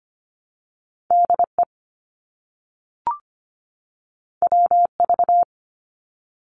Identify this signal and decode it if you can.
Morse code, heard as a keyed tone, about 25 wpm, 700 Hz. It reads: DE WV
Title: US States 25wpm